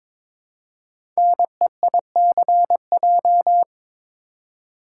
Morse code, heard as a keyed tone, about 22 wpm, 700 Hz. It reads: NEICJ
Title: Code Groups - 5 Letters - 22wpm